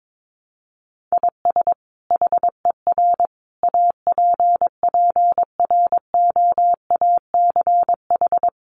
Morse code, heard as a keyed tone, about 22 wpm, 700 Hz. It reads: IS HER APPROACH
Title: Sentences from Top 1500 Words 22wpm